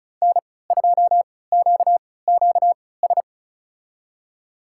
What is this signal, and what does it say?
Morse code, heard as a keyed tone, about 36 wpm, spaced as for 20 wpm, 700 Hz. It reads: N2QQS
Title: USA Call Signs - 3x Character Spacing - 35wpm